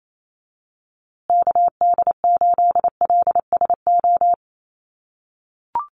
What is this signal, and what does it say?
Morse code, heard as a keyed tone, about 28 wpm, 700 Hz. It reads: KD8LSO
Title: USA Call Signs Repeated 2x 28wpm